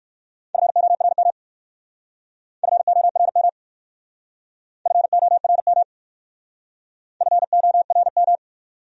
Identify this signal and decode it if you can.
Morse code, heard as a keyed tone, about 54 wpm, 700 Hz. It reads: FORM FORM FORM FORM
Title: Speed-Racing Top 200 Words 45wpm